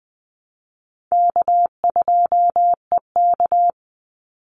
Morse code, heard as a keyed tone, about 20 wpm, 700 Hz. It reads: K2EK
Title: Easy Call Signs Repeated 2x - 20wpm